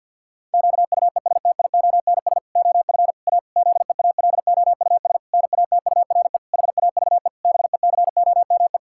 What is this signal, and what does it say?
Morse code, heard as a keyed tone, about 47 wpm, 700 Hz. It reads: QUESTIONS OF A ZEALOUS NATURE HAVE BECOME
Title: Pangram Challenge v2 - 50wpm